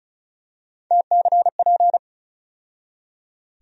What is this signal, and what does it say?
Morse code, heard as a keyed tone, about 36 wpm, 700 Hz. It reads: TCP
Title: USA Call Sign Postfixes - 3 Letters 35wpm